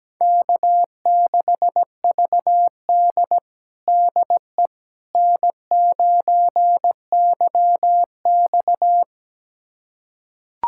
Morse code, heard as a keyed tone, about 17 wpm, 700 Hz. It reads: K6VD DE N9YX
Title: USA Callsign DE Callsign - 4 Characters - 17wpm